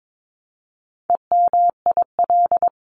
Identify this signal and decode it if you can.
Morse code, heard as a keyed tone, about 22 wpm, 700 Hz. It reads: EMIL
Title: CWT Exchange 22wpm